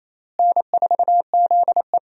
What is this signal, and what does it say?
Morse code, heard as a keyed tone, about 27 wpm, 700 Hz. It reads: N4ZE
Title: USA Call Signs - 4 Characters - Repeated 3x - 28wpm